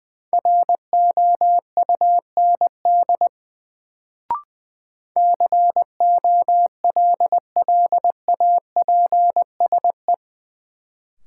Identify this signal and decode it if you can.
Morse code, heard as a keyed tone, about 21 wpm, 700 Hz. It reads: ROUND COLLAPSE
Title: Sets of 5 Words Encore - New Word Drill - 20wpm